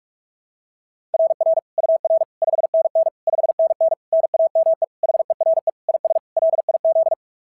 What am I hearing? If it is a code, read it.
Morse code, heard as a keyed tone, about 46 wpm, 690 Hz. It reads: RR UR 5NN 5NN NAME HERE IS LIZ
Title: SOTA QSO Practice 45wpm